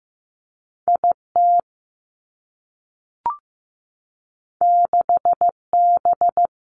Morse code, heard as a keyed tone, about 15 wpm, 700 Hz. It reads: I T 6 B
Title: Two Challenging Letter-Number Combinations 15wpm